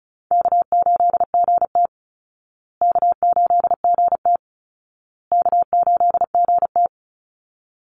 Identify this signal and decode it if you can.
Morse code, heard as a keyed tone, about 34 wpm, 700 Hz. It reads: K8GT K8GT K8GT
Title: Call Signs Repeated 2x - 35wpm